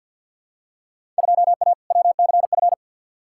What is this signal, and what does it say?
Morse code, heard as a keyed tone, about 47 wpm, 700 Hz. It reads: 2A WCF
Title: Field Day Exchanges - 50wpm